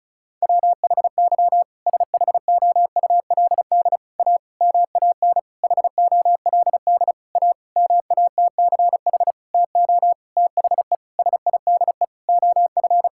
Morse code, heard as a keyed tone, about 34 wpm, 700 Hz. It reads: WHY SHOULD A MAN HOLD A MATCH TO THE SIDE OF
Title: Sentences from Top 1600 Words 35wpm